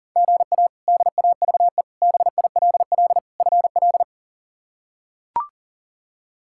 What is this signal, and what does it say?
Morse code, heard as a keyed tone, about 40 wpm, 700 Hz. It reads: GA DAVE BILL FL
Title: K1USN SST QSO Practice - 40wpm